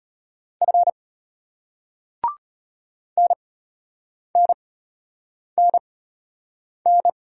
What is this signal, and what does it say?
Morse code, heard as a keyed tone, about 34 wpm, spaced as for 8 wpm, 700 Hz. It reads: F NNNN